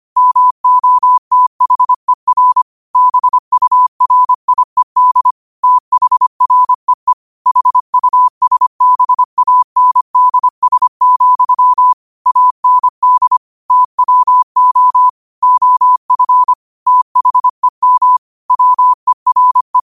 Morse code, heard as a keyed tone, about 25 wpm, 1000 Hz. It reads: MOTHER BURIED THREE HUSBANDS, AND TWO OF THEM WERE
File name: qotdfiles / 25wpm